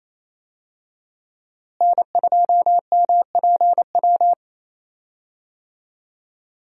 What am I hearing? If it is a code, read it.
Morse code, heard as a keyed tone, about 28 wpm, 700 Hz. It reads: N2MPW
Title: USA Call Signs - 5 Characters - Repeated 3x - 28wpm